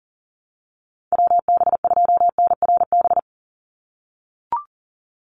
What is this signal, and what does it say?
Morse code, heard as a keyed tone, about 40 wpm, 700 Hz. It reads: WB2NRB
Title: USA Call Signs 40wpm